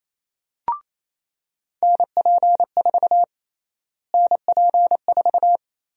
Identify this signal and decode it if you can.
Morse code, heard as a keyed tone, about 28 wpm, 700 Hz. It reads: NP4 NP4
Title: USA Call Sign Prefixes - Repeated 2x - 28wpm